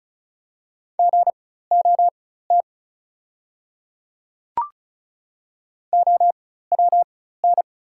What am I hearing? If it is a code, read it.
Morse code, heard as a keyed tone, about 36 wpm, spaced as for 16 wpm, 700 Hz. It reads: GOT OWN
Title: 4x Character Spacing - Top 300 Words - 35wpm